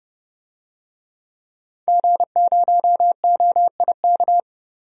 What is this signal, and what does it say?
Morse code, heard as a keyed tone, about 32 wpm, 700 Hz. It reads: G0OIK
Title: DXCC-100 Call Signs - 5 Characters - Repeated 2x - 30wpm